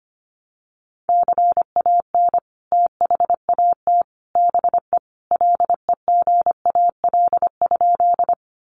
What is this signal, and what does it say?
Morse code, heard as a keyed tone, about 25 wpm, 700 Hz. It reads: CAN THAT BE LEGAL?
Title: Sentences from Top 1200 Words 25wpm